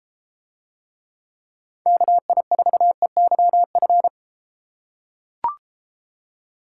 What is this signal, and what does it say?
Morse code, heard as a keyed tone, about 33 wpm, 700 Hz. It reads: KI4EYF